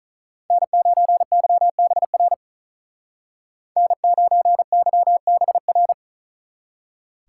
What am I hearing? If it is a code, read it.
Morse code, heard as a keyed tone, about 45 wpm, 700 Hz. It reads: N9YBR N9YBR